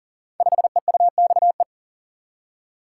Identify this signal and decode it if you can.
Morse code, heard as a keyed tone, about 38 wpm, 700 Hz. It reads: 5EUXE
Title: Code Groups - 5 Letter:Numbers - 40wpm